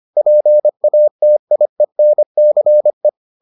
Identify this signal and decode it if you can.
Morse code, heard as a keyed tone, about 25 wpm, 600 Hz. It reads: PATIENCE